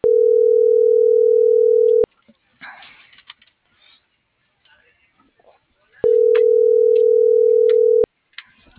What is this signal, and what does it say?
Ringback tone (North America), 440 + 480 Hz.